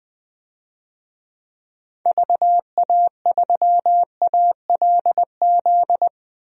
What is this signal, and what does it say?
Morse code, heard as a keyed tone, about 20 wpm, 700 Hz. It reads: VA3ALZ
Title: DXCC-100 Call Signs - 6 Characters - 20wpm